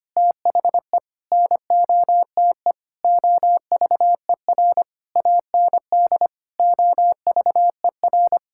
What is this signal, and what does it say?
Morse code, heard as a keyed tone, about 25 wpm, 700 Hz. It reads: THE NOTE OVER AND OVER
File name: practice / 25wpm / Sentences with 4-Word Phrases - 25wpm